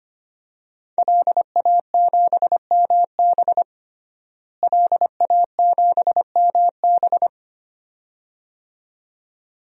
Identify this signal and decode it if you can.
Morse code, heard as a keyed tone, about 25 wpm, 700 Hz. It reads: LA7MB LA7MB
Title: DXCC-100 Call Signs - 5 Characters - Repeated 2x - 25wpm